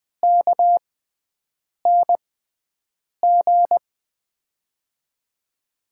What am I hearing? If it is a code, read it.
Morse code, heard as a keyed tone, about 20 wpm, spaced as for 7 wpm, 700 Hz. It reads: KNG
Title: USA Call Signs - 5 Characters - 6x Character Spacing - 20wpm